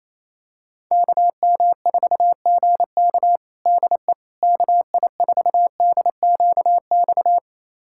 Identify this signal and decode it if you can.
Morse code, heard as a keyed tone, about 27 wpm, 700 Hz. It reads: KM4GK DE KI4DQX
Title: USA Callsign DE Callsign 28wpm